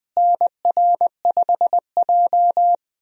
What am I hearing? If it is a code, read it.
Morse code, heard as a keyed tone, about 20 wpm, 700 Hz. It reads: NR5J
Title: USA Call Signs - 4 Characters - Repeated 2x - 20wpm